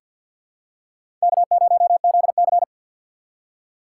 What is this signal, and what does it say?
Morse code, heard as a keyed tone, about 53 wpm, 700 Hz. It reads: K0ZC
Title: USA Call Signs - 4 Characters - 3x Character Spacing - 50wpm